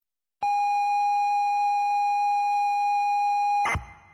ZX Spectrum tape loading.